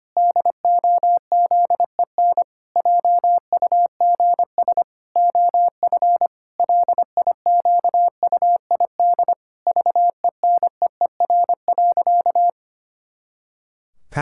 Morse code, heard as a keyed tone, about 25 wpm, 700 Hz. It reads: DOZEN JUGS OF LIQUID VENEER.
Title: Pangram Challenge v2 - 25wpm